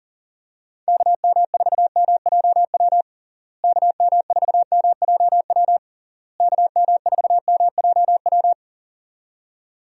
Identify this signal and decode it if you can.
Morse code, heard as a keyed tone, about 37 wpm, 700 Hz. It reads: KM4MJW KM4MJW KM4MJW
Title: USA Call Signs - 6 Characters - Repeated 3x - 40wpm